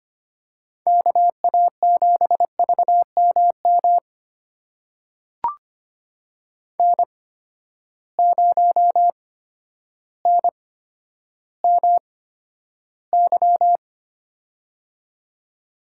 Morse code, heard as a keyed tone, about 25 wpm, spaced as for 11 wpm, 700 Hz. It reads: KA7VMM N0NMY
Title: USA Call Signs - 8x Character Spacing - 25wpm